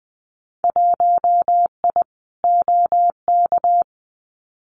Morse code, heard as a keyed tone, about 20 wpm, 700 Hz. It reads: 1I OK
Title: Winter Field Day Exchanges Encore 20wpm